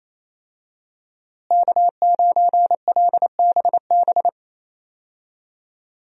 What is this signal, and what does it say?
Morse code, heard as a keyed tone, about 28 wpm, 700 Hz. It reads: K9LBB
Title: USA Call Signs - 5 Characters - Repeated 2x - 28wpm